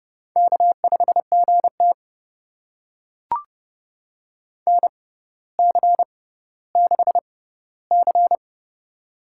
Morse code, heard as a keyed tone, about 30 wpm, spaced as for 15 wpm, 700 Hz. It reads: K5GT NC6C
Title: USA Call Signs - 6x Character Spacing - 30wpm